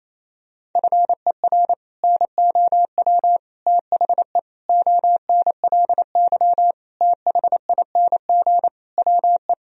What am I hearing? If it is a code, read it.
Morse code, heard as a keyed tone, about 28 wpm, 700 Hz. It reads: FER NOW THE ONLY THING WE
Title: Sentences from Top 200 Words and Abbreviations 28wpm